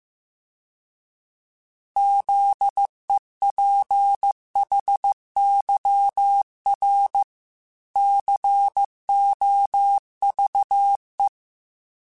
Morse code, heard as a keyed tone, about 15 wpm, 800 Hz. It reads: ZEPHYR COVE